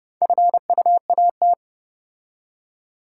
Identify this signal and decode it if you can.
Morse code, heard as a keyed tone, about 28 wpm, 700 Hz. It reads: FUAT